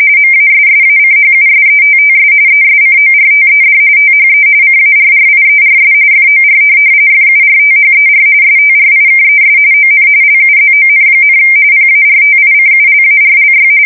RTTY, about 45.45 baud. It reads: MASSACHUSETTS ESCORTED THE CARRIERS FOR FURTHER STRIKES ON LUZON, PARTICULARLY
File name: fsk170 / USS-Massachusetts